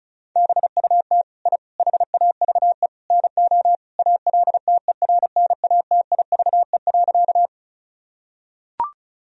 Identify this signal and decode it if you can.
Morse code, heard as a keyed tone, about 34 wpm, 700 Hz. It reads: BUT I HAVE NO ALTERNATIVE.
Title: Sentences from Top 1000 Words - Review Part 4 - 35wpm